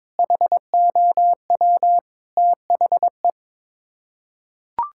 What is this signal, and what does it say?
Morse code, heard as a keyed tone, about 22 wpm, 700 Hz. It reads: HOW THE
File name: practice / 22wpm / Sets of 8 Words - 22wpm